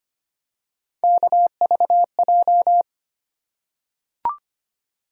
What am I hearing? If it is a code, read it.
Morse code, heard as a keyed tone, about 25 wpm, 700 Hz. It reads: KVJ
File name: Three Letter-Number Combinations - 25wpm